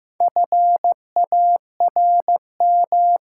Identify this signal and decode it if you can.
Morse code, heard as a keyed tone, about 15 wpm, 700 Hz. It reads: FARM
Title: Sentences from Top 1000 Words - Review Part 2 - New Word Drill - 15wpm